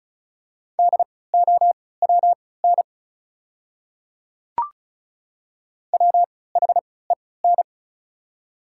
Morse code, heard as a keyed tone, about 36 wpm, spaced as for 20 wpm, 690 Hz. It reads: DOWN WHEN
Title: 3x Character Spacing - Top 200 Words - 35wpm